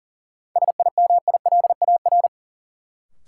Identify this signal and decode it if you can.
Morse code, heard as a keyed tone, about 40 wpm, 700 Hz. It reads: SIMILAR
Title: Sentences from Top 1000 Words - Review Part 5 - New Word Drill - 40wpm